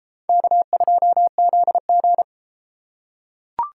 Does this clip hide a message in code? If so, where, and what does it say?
Yes: Morse code from 0.3 to 2.2 s, heard as a keyed tone, about 32 wpm, 700 Hz. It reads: K2ZG